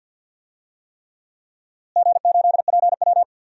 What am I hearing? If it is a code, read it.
Morse code, heard as a keyed tone, about 51 wpm, 700 Hz. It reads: G8PW
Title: DXCC-100 Call Signs - 4 Characters - Repeated 3x - 50wpm